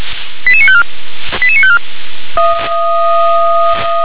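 Five-tone selective call (ZVEI), heard as quick runs of five tones, tones of about 70 ms. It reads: 88954 88954
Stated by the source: Five-tone ZVEI selective call received by SDR.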